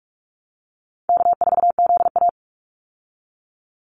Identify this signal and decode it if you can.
Morse code, heard as a keyed tone, about 44 wpm, 700 Hz. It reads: K4ZA
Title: Easy Call Signs Repeated 3x - 45wpm